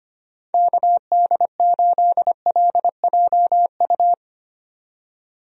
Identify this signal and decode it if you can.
Morse code, heard as a keyed tone, about 25 wpm, 700 Hz. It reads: KD8LJU
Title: USA Call Signs - 6 Characters - 25wpm